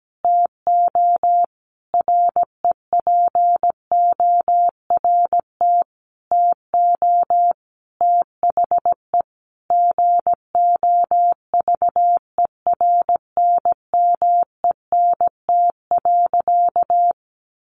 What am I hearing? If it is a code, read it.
Morse code, heard as a keyed tone, about 17 wpm, 700 Hz. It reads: TO REPORT TO THE GOVERNMENT.
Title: A Sound Of Thunder 17wpm